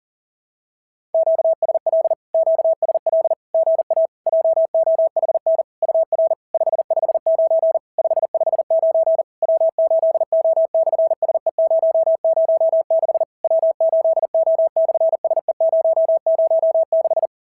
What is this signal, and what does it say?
Morse code, heard as a keyed tone, about 40 wpm, 650 Hz. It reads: QSL QSL GA JOHN UR 559 559 W8O/SE006 W8O/SE006
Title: SOTA QSO Practice 40wpm